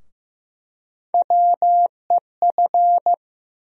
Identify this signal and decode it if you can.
Morse code, heard as a keyed tone, about 15 wpm, 700 Hz. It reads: WEF
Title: USA Call Sign Postfixes - 3 Letters 15wpm